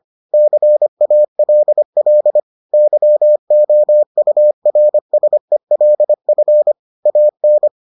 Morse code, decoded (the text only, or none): CALL YOURSELF AN